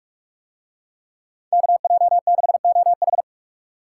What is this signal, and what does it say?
Morse code, heard as a keyed tone, about 45 wpm, 700 Hz. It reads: KJ6OH